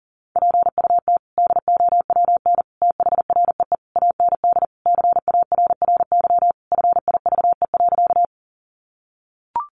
Morse code, heard as a keyed tone, about 40 wpm, 700 Hz. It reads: PUT DOWN THREE AND CARRY FIVE.
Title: Sentences from Top 400 Words 40wpm